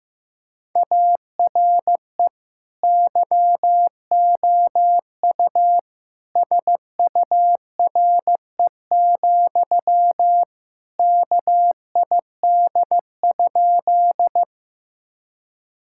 Morse code, heard as a keyed tone, about 15 wpm, 700 Hz. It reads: ARE YOU SURE, KID?
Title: Sentences from Top 1600 Words 15wpm